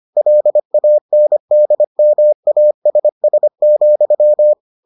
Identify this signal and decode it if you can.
Morse code, heard as a keyed tone, about 25 wpm, 600 Hz. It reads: LANDMASS,